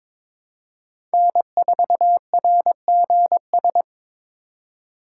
Morse code, heard as a keyed tone, about 22 wpm, 700 Hz. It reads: N4RGS
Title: USA Call Signs - 5 Characters - 2x Character Spacing - 22wpm